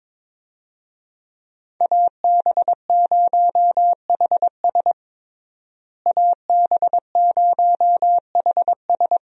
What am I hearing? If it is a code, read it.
Morse code, heard as a keyed tone, about 22 wpm, 700 Hz. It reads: AB0HS AB0HS
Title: USA Call Signs - 5 Characters - Repeated 2x - 22wpm